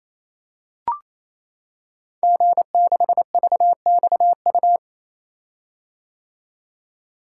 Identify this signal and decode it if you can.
Morse code, heard as a keyed tone, about 28 wpm, 700 Hz. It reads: G6VXU
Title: Code Groups - 5 Letter:Numbers - 28wpm